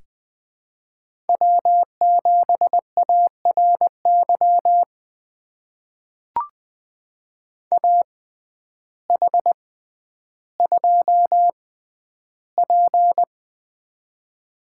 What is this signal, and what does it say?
Morse code, heard as a keyed tone, about 20 wpm, spaced as for 11 wpm, 700 Hz. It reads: W7ARY AH2P
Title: USA Call Signs - 6x Character Spacing - 20wpm